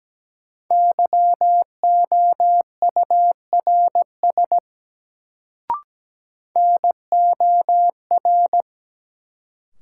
Morse code, heard as a keyed tone, about 17 wpm, 700 Hz. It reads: YOURS NOR
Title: Sentences from Top 1500 Words - New Word Drill - 17wpm